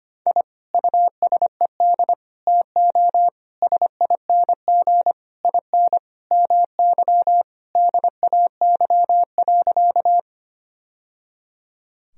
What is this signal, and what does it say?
Morse code, heard as a keyed tone, about 25 wpm, 700 Hz. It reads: I USED TO SING IN MY DAY.
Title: Sentences from Top 1000 Words - Review Part 2 - 25wpm